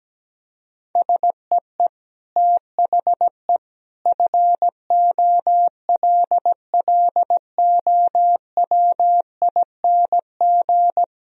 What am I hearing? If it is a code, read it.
Morse code, heard as a keyed tone, about 17 wpm, 700 Hz. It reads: SEE THE FOLLOWING
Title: Sentences from Top 1000 Words - Review Part 1 - 17wpm